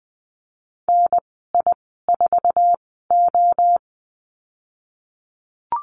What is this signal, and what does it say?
Morse code, heard as a keyed tone, about 20 wpm, spaced as for 15 wpm, 700 Hz. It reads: NI4O
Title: 088 Lesson 15g - Number 5 - Call Signs 20wpm